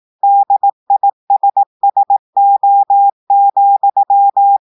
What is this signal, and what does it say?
Morse code, heard as a keyed tone, about 18 wpm, 800 Hz. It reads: DISSO,
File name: audios / portugais / portugais texte en clair 2 vitesse 18 mots minutes